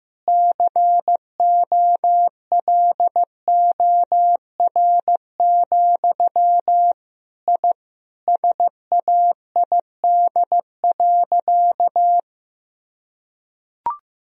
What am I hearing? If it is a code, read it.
Morse code, heard as a keyed tone, about 15 wpm, 700 Hz. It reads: COLOR, I SAID.
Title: Sentences from Top 1000 Words - Review Part 2 - 15wpm